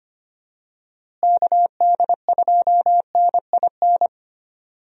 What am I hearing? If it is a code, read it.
Morse code, heard as a keyed tone, about 25 wpm, 700 Hz. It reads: KD2NIN
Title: USA Call Signs - 6 Characters - Repeated 3x - 25wpm